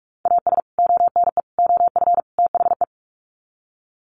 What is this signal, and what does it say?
Morse code, heard as a keyed tone, about 44 wpm, 700 Hz. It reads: AS ONE OF THE
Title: Sets of 4 Words 45wpm